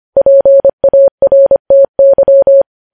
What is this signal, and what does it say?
Morse code, heard as a keyed tone, about 25 wpm, 550 Hz. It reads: PARTY
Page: news headlines in morse code at 25 wpm